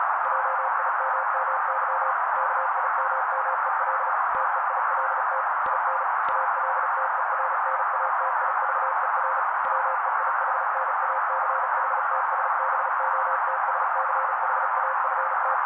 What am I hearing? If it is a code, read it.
Morse code, heard as a keyed tone, about 36 wpm, 530 Hz. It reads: JIMMY KIMMEL TEILT IN ALTERNATIVER WEIHNACHTSBOTSCHAFT
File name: daily / 35wpm noise